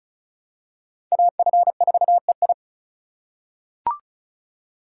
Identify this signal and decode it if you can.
Morse code, heard as a keyed tone, about 34 wpm, 700 Hz. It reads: AF4EI
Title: USA Call Signs - 5 Characters - 4x Character Spacing - 35wpm